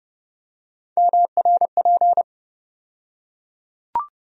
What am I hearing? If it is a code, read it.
Morse code, heard as a keyed tone, about 32 wpm, 700 Hz. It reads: MRP
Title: USA Call Sign Postfixes - 3 Letters 30wpm